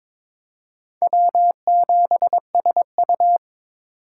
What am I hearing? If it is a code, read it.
Morse code, heard as a keyed tone, about 26 wpm, 700 Hz. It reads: W7SU
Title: Speed Racing - USA 1x2 Call Signs 22wpm